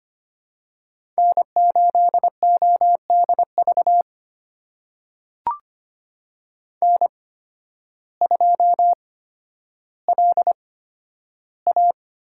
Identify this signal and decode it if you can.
Morse code, heard as a keyed tone, about 25 wpm, spaced as for 12 wpm, 700 Hz. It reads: N8ODV N2LA
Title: USA Call Signs - 8x Character Spacing - 25wpm